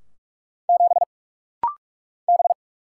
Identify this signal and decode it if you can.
Morse code, heard as a keyed tone, about 45 wpm, 700 Hz. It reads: 7 B
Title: Single Letter-Number - Rapid-Fire 45wpm